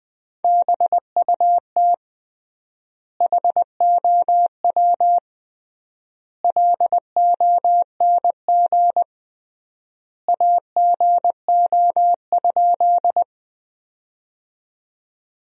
Morse code, heard as a keyed tone, about 20 wpm, 700 Hz. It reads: BUT HOW LONG AGO?
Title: 3x Word-Spacing - Sentences from Top 500 Words - 20wpm